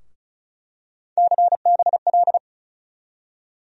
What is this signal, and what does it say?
Morse code, heard as a keyed tone, about 36 wpm, 700 Hz. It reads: CBL
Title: USA Call Sign Postfixes - 3 Letters 35wpm